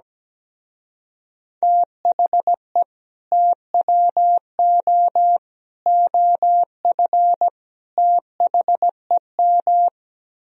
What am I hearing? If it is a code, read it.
Morse code, heard as a keyed tone, about 17 wpm, 700 Hz. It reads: THE TWO OF THEM
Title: Speed-Racing - Sets of 4 Words 17wpm